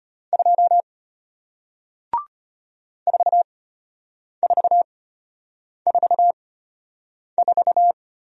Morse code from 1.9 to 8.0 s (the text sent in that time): 4444